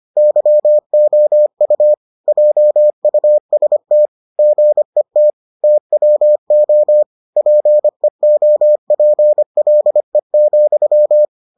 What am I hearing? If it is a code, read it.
Morse code, heard as a keyed tone, about 25 wpm, 600 Hz. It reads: YOU JUST GET TWO PEOPLE,